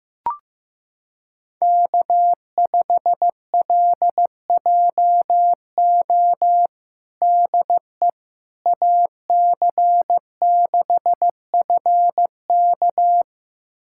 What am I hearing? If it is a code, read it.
Morse code, heard as a keyed tone, about 15 wpm, 700 Hz. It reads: K5LJO DE AC6FK
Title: USA Callsign DE Callsign - 5 Characters - 15wpm